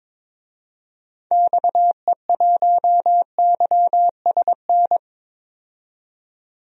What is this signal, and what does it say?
Morse code, heard as a keyed tone, about 22 wpm, 700 Hz. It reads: XE1YSN
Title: DXCC-100 Call Signs - 6 Characters - 22wpm